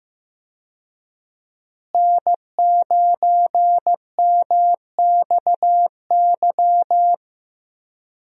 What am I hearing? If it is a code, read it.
Morse code, heard as a keyed tone, about 15 wpm, 700 Hz. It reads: N9MXY
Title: USA Call Signs - 5 Characters - Repeated 3x - 15wpm